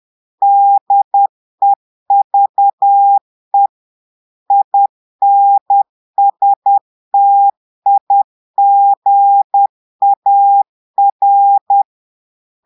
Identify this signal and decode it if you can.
Morse code, heard as a keyed tone, about 10 wpm, 800 Hz. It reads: DEVE INSTIGAR
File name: portugais texte en clair 2 vitesse 10 mots minutes